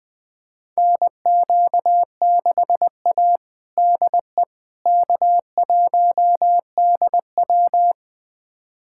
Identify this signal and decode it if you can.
Morse code, heard as a keyed tone, about 21 wpm, 700 Hz. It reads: NQ6A DE K1DW
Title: USA Callsign DE Callsign - 4 Characters - 20wpm